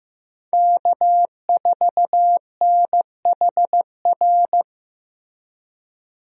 Morse code, heard as a keyed tone, about 15 wpm, 700 Hz. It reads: K4NHR
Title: USA Call Signs - 5 Characters - Repeated 3x - 15wpm